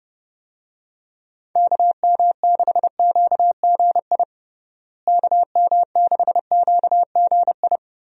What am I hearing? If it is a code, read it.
Morse code, heard as a keyed tone, about 30 wpm, 700 Hz. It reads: KM6QGI KM6QGI
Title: USA Call Signs - 6 Characters - Repeated 2x - 30wpm